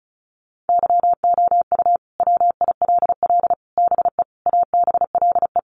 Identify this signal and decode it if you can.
Morse code, heard as a keyed tone, about 34 wpm, 700 Hz. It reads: YOU WILL BE ABLE
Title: Sets of 5 Words Encore2 35wpm